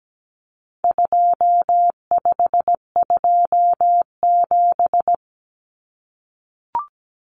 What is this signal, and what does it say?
Morse code, heard as a keyed tone, about 17 wpm, 700 Hz. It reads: 2527
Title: Numbers - 4-Digit 17wpm